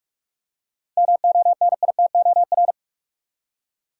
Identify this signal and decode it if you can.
Morse code, heard as a keyed tone, about 41 wpm, 700 Hz. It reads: MONITOR
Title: Seven Letter Words - 45wpm